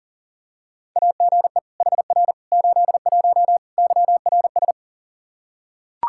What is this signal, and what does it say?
Morse code, heard as a keyed tone, about 40 wpm, 700 Hz. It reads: AGE HR 81 YRS
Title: Practice QSO - Exchange 3 - More Info - 40wpm